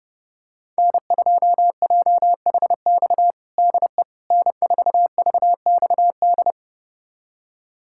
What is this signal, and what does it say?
Morse code, heard as a keyed tone, about 32 wpm, 700 Hz. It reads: N2JHX DE N4VXD
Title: USA Callsign DE Callsign - 5 Characters - 30wpm